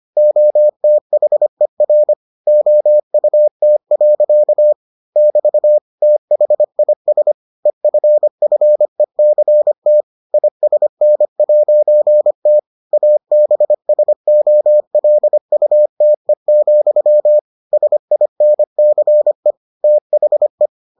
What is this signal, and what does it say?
Morse code, heard as a keyed tone, about 25 wpm, 600 Hz. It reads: OTHER OUT. <BT> THIS EFFECT ISN'T ABSOLUTE, SINCE THE